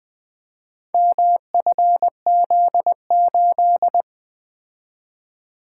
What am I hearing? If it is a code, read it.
Morse code, heard as a keyed tone, about 20 wpm, 700 Hz. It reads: MFZ8
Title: Four Letter-Number Combinations - 20wpm